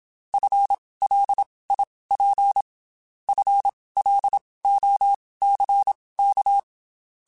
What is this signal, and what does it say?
Morse code, heard as a keyed tone, about 27 wpm, spaced as for 19 wpm, 800 Hz. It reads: FLIP FLOCK
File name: Spaced / FL Runs sp mp3s / FL Runs sp @25